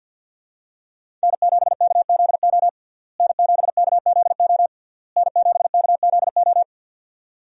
Morse code, heard as a keyed tone, about 51 wpm, 700 Hz. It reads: N7KZO N7KZO N7KZO
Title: USA Call Signs - 5 Characters - Repeated 3x - 50wpm